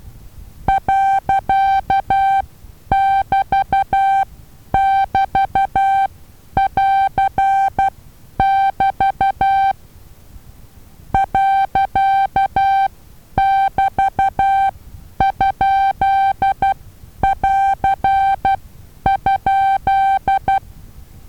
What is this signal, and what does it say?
Morse code, heard as a keyed tone, about 12 wpm, spaced as for 9 wpm, 780 Hz. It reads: .<BT><BT><AR><BT> .<BT>?<AR>?
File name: Taviro HA6VA B